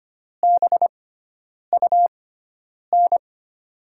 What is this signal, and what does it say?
Morse code, heard as a keyed tone, about 25 wpm, spaced as for 9 wpm, 700 Hz. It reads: BUN
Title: USA Call Signs - 6x Character Spacing - 25wpm